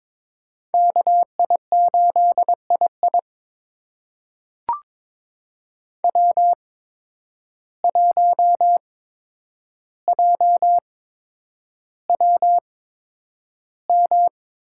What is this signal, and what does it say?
Morse code, heard as a keyed tone, about 22 wpm, spaced as for 9 wpm, 700 Hz. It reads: KI8II W1JWM
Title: USA Call Signs - 8x Character Spacing - 22wpm